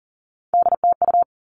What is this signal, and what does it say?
Morse code, heard as a keyed tone, about 40 wpm, 700 Hz. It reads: DTU
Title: All Cut Numbers - 3-Digit 40wpm